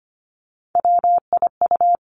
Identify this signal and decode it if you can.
Morse code, heard as a keyed tone, about 25 wpm, 700 Hz. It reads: WIU
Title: Three Challenging Letter-Number Combinations 25wpm